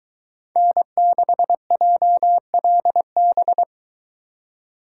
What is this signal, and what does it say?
Morse code, heard as a keyed tone, about 22 wpm, 700 Hz. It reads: N6JLB